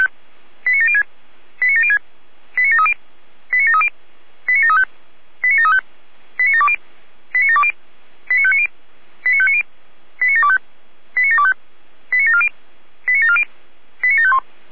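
Five-tone selective call (ZVEI), heard as quick runs of five tones, tones of about 70 ms. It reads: 89706 89706 89730 89730 89735 89735 89720 89720 89590 89590 89725 89725 89740 89740 89751
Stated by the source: ZVEI five-tone call received by SDR.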